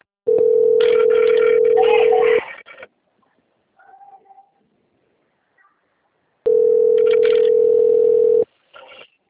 Ringback tone (North America), 440 + 480 Hz.